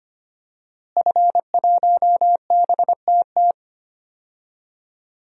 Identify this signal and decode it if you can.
Morse code, heard as a keyed tone, about 25 wpm, 700 Hz. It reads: F1BTT
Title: DXCC-100 Call Signs - 5 Characters - Repeated 3x - 25wpm